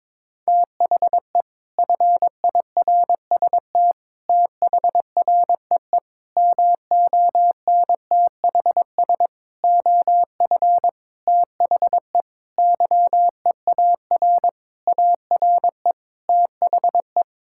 Morse code, heard as a keyed tone, about 21 wpm, 700 Hz. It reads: THE FIRST THREE MONTHS OF THE YEAR ARE THE
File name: practice / 22wpm / Sentences with 7-Word Phrases - 22wpm